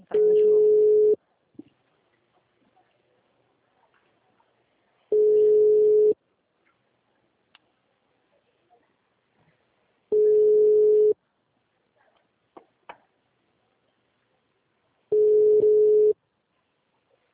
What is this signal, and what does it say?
Ringback tone (Europe, 425 Hz).